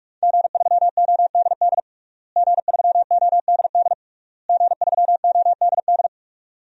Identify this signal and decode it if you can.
Morse code, heard as a keyed tone, about 41 wpm, 700 Hz. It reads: G3ODD G3ODD G3ODD
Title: DXCC-100 Call Signs - 5 Characters - Repeated 3x - 45wpm